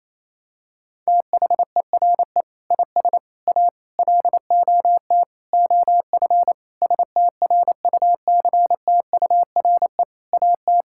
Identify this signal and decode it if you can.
Morse code, heard as a keyed tone, about 28 wpm, 700 Hz. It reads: THERE IS A LOT OF STRUCTURE AT
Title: Sentences from Top 1000 Words - Review Part 6 - 28wpm